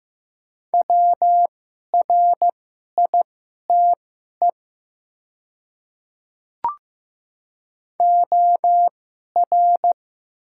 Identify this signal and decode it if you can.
Morse code, heard as a keyed tone, about 15 wpm, spaced as for 11 wpm, 700 Hz. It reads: WRITE OR
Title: 2x Character Spacing - Top 100 Words - 15wpm